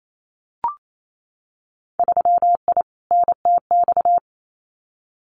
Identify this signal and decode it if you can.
Morse code, heard as a keyed tone, about 28 wpm, 700 Hz. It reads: 3I NTX
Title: Winter Field Day Exchanges 28wpm